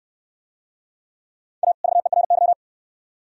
Morse code, heard as a keyed tone, about 67 wpm, 700 Hz. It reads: A VERY